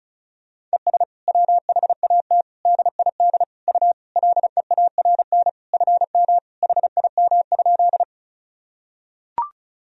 Morse code, heard as a keyed tone, about 36 wpm, 700 Hz. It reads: ES WHAT DID U LEARN FM HIM?
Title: Sentences from Top 300 Words and Abbreviations 35wpm